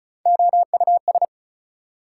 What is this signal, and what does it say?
Morse code, heard as a keyed tone, about 34 wpm, 700 Hz. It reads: OUS
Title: Prefixes and Suffixes 35wpm